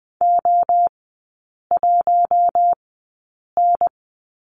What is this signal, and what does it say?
Morse code, heard as a keyed tone, about 20 wpm, spaced as for 8 wpm, 700 Hz. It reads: O1N
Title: 152 Lesson 25e - Letter p - Three Characters 20wpm